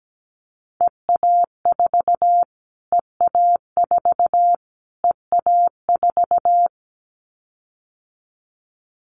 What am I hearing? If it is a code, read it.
Morse code, heard as a keyed tone, about 17 wpm, 700 Hz. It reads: EA4 EA4 EA4
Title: Callsign Prefixes for Top 25 DXCC - Repeated 3x - 17wpm